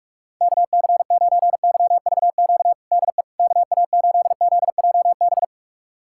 Morse code, heard as a keyed tone, about 42 wpm, 690 Hz. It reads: KC9YVQ DE KA8ZJB
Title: USA Callsign DE Callsign - 6 Characters - 45wpm